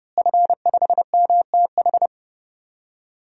Morse code, heard as a keyed tone, about 30 wpm, 700 Hz. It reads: F5MTH
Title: DXCC-100 Call Signs - 5 Characters - Repeated 2x - 30wpm